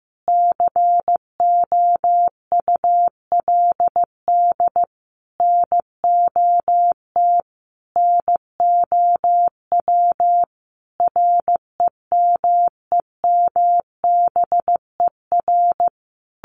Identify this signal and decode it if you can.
Morse code, heard as a keyed tone, about 15 wpm, 700 Hz. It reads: COULD NOT NOW REMEMBER
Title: Sentences from Top 1200 Words 15wpm